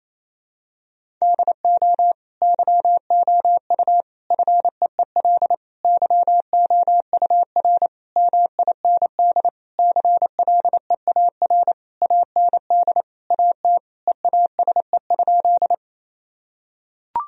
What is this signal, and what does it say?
Morse code, heard as a keyed tone, about 28 wpm, 700 Hz. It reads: DO YOU FEEL YOUR MIND CLEAR AND AT EASE?
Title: Sentences from Top 1000 Words - Review Part 1 - 28wpm